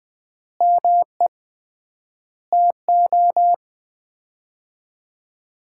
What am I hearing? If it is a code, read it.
Morse code, heard as a keyed tone, about 20 wpm, 700 Hz. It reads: ME TO
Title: 3x Word-Spacing - Sets of 5 Words - 20wpm